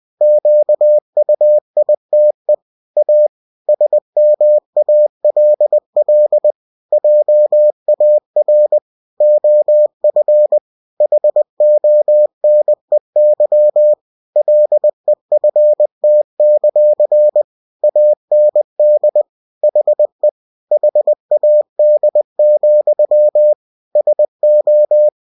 Morse code, heard as a keyed tone, about 20 wpm, 600 Hz. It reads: QUITE A SMALL JAR OF HONEY LEFT; AND HE HAD, SO
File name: cw-books / winnie-the-pooh / winnie-20wpm-page-0049